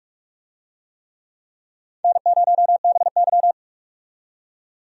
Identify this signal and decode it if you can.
Morse code, heard as a keyed tone, about 46 wpm, 700 Hz. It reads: N0BY
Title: USA Call Signs - 4 Characters - 45wpm